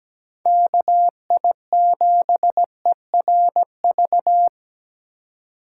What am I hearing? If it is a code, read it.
Morse code, heard as a keyed tone, about 20 wpm, 700 Hz. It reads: KI7ERV
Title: Speed Racing - USA 2x3 Call Signs 17wpm